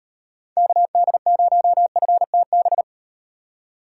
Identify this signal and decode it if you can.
Morse code, heard as a keyed tone, about 36 wpm, 700 Hz. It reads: KD0FTB